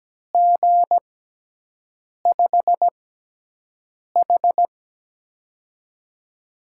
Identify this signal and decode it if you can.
Morse code, heard as a keyed tone, about 17 wpm, spaced as for 6 wpm, 700 Hz. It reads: G5H
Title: USA Call Signs - 6x Character Spacing - 17wpm